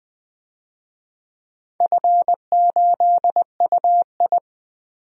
Morse code, heard as a keyed tone, about 20 wpm, 700 Hz. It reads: F8UI
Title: DXCC-100 Call Signs - 4 Characters - 20wpm